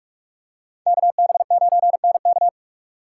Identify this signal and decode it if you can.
Morse code, heard as a keyed tone, about 48 wpm, 700 Hz. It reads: KB9NK